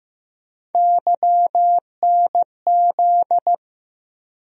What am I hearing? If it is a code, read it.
Morse code, heard as a keyed tone, about 15 wpm, 700 Hz. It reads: YNZ
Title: USA Call Sign Postfixes - 3 Letters 15wpm